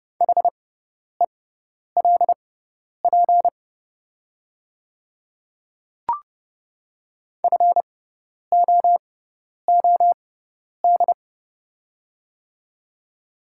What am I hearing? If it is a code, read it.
Morse code, heard as a keyed tone, about 29 wpm, spaced as for 10 wpm, 700 Hz. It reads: HELP FOOD
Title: 6x Character Spacing - Top 300 Words - 30wpm